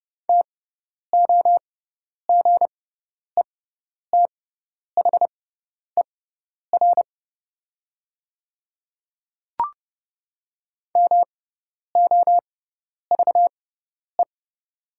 Morse code, heard as a keyed tone, about 31 wpm, spaced as for 10 wpm, 700 Hz. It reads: TOGETHER MOVE
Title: 6x Character Spacing - Top 500 Words - 30wpm